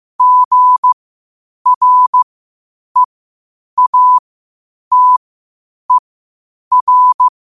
Morse code, heard as a keyed tone, about 15 wpm, spaced as for 8 wpm, 1000 Hz. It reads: GREATER
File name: qotdfiles / 8wpm